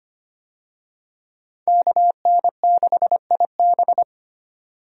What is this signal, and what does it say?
Morse code, heard as a keyed tone, about 25 wpm, 700 Hz. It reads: KN6IB
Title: USA Call Signs - 5 Characters - 25wpm